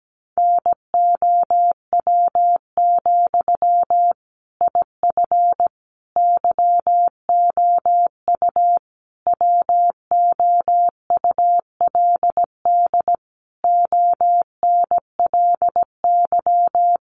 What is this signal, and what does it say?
Morse code, heard as a keyed tone, about 17 wpm, 700 Hz. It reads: NOW, IF YOU WOULD ONLY
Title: Sentences from Top 900 Words 17wpm